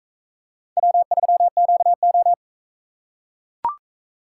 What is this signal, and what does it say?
Morse code, heard as a keyed tone, about 40 wpm, 700 Hz. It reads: W3QO